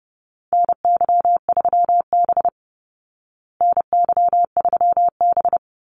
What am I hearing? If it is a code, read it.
Morse code, heard as a keyed tone, about 30 wpm, 700 Hz. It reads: NY3B NY3B
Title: Call Signs Repeated 3x - 30wpm